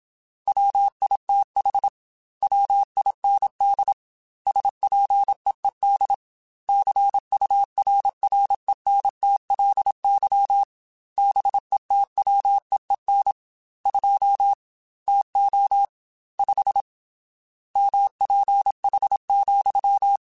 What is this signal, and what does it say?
Morse code, heard as a keyed tone, about 27 wpm, 800 Hz. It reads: WITH WIND SPEED CURRENTLY BETWEEN 2 TO 5 MPH,
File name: Weather Sets / Weather Sets-mp3s / Weather 5 12 @25